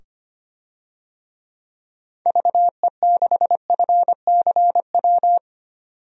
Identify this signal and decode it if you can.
Morse code, heard as a keyed tone, about 25 wpm, 700 Hz. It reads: VE6FCW
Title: DXCC-100 Call Signs - 6 Characters - 25wpm